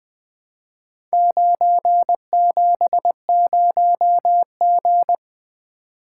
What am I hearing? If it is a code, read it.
Morse code, heard as a keyed tone, about 20 wpm, 700 Hz. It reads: 970G
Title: Four Letter-Number Combinations - 20wpm